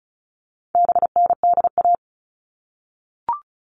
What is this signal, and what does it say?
Morse code, heard as a keyed tone, about 34 wpm, 700 Hz. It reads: BNDA
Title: All Cut Numbers - 4-Digit 35wpm